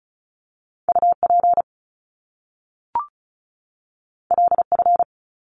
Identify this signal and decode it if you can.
Morse code, heard as a keyed tone, about 36 wpm, 700 Hz. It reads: UP LF
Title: Top Two Letter Combinations in Words 35wpm